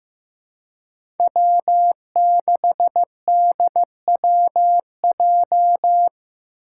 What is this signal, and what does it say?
Morse code, heard as a keyed tone, about 15 wpm, 700 Hz. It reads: W6DWJ
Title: Five Challenging Letter-Number Combinations 15wpm